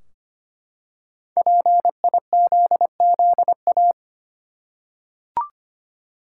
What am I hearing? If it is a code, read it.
Morse code, heard as a keyed tone, about 25 wpm, 700 Hz. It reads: PIZZA
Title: Five Letter Words - 25wpm